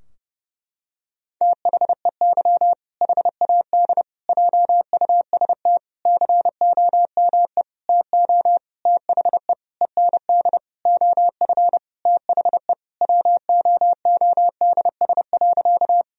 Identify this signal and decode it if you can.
Morse code, heard as a keyed tone, about 30 wpm, 700 Hz. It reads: THEY HAD JUST COME TO THE END OF THE WOODS.
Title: Sentences from Top 1000 Words - Review Part 2 - 30wpm